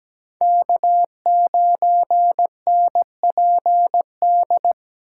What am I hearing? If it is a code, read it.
Morse code, heard as a keyed tone, about 17 wpm, 700 Hz. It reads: K9NPD
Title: USA Call Signs - 5 Characters - 3x Character Spacing - 17wpm